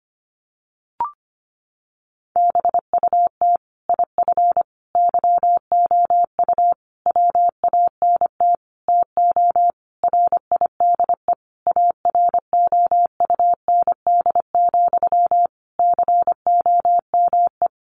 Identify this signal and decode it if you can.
Morse code, heard as a keyed tone, about 25 wpm, 700 Hz. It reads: BUT IF YOU WANT TO RIDE AROUND, COME
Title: Sentences from Top 1200 Words 25wpm